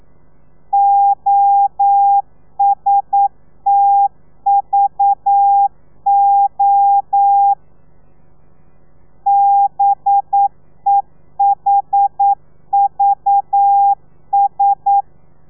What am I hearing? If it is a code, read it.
Morse code, heard as a keyed tone, about 9 wpm, 800 Hz. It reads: OSTVO BEHVS
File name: telegrafia / i2tfj / Lez4 40